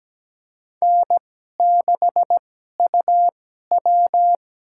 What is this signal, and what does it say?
Morse code, heard as a keyed tone, about 17 wpm, spaced as for 12 wpm, 700 Hz. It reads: N6UW
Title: USA Call Signs - 4 Characters - 2x Character Spacing - 17wpm